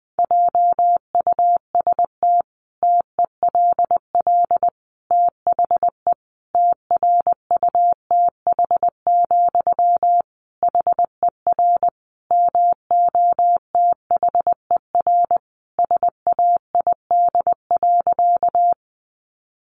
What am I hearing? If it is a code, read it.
Morse code, heard as a keyed tone, about 19 wpm, 700 Hz. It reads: JUST TELL THE TRUTH, HER MOTHER SAID.
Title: Sentences from Top 800 Words 20wpm